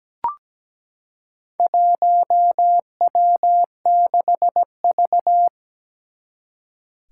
Morse code, heard as a keyed tone, about 17 wpm, 700 Hz. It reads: 1W6V
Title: Four Challenging Letter-Number Combinations 17wpm